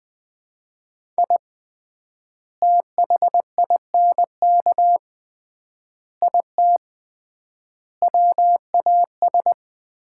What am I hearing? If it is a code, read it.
Morse code, heard as a keyed tone, about 21 wpm, 700 Hz. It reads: I THINK IT WAS
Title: 3x Word-Spacing - Sets of 5 Words - 20wpm